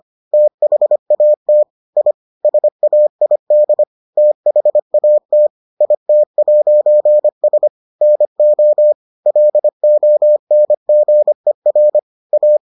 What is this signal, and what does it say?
Morse code, heard as a keyed tone, about 25 wpm, 600 Hz. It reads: THAT I SAID THAT IT'S NO LONGER A